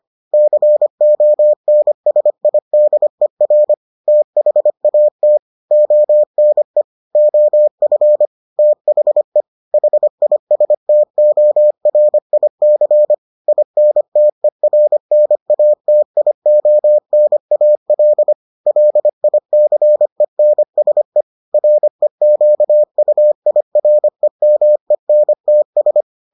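Morse code, heard as a keyed tone, about 25 wpm, 600 Hz. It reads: CONSIDER THAT ONE OF THE HISTORIC INTERNATIONAL LICENSE REQUIREMENTS
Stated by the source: Morse edition of the Foundations of Amateur Radio episodes.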